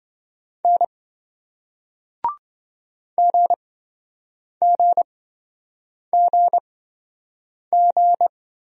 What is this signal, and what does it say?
Morse code, heard as a keyed tone, about 26 wpm, spaced as for 7 wpm, 700 Hz. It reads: N GGGG